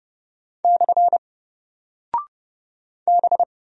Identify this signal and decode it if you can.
Morse code, heard as a keyed tone, about 31 wpm, 700 Hz. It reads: / B
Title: Single Character (Letter:Number:Punctuation:Prosign) 30wpm